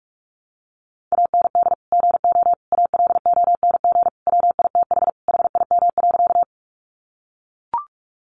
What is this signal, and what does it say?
Morse code, heard as a keyed tone, about 44 wpm, 700 Hz. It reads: AND GO ALONG WITH HIM.
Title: Sentences from Top 700 Words 45wpm